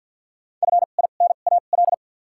Morse code, heard as a keyed tone, about 53 wpm, spaced as for 37 wpm, 700 Hz. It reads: FINAL
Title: 2x Character Spacing - Top 500 Words - 50wpm